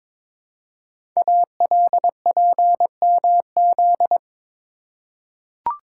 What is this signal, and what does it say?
Morse code, heard as a keyed tone, about 22 wpm, 700 Hz. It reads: ALPMZ
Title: Code Groups - 5 Letter:Numbers - 22wpm